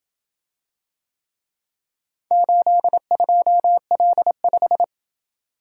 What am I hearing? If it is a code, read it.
Morse code, heard as a keyed tone, about 27 wpm, 700 Hz. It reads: 82L5